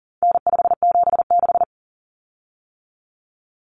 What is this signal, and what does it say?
Morse code, heard as a keyed tone, about 40 wpm, 700 Hz. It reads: N576
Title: Common Cut Numbers - 4-Digit 40wpm